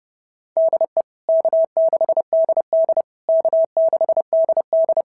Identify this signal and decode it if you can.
Morse code, heard as a keyed tone, about 28 wpm, 650 Hz. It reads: DE K6DD K6DD
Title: SOTA QSO Practice 30wpm